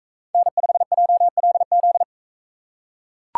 Morse code, heard as a keyed tone, about 42 wpm, 700 Hz. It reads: N5JLZ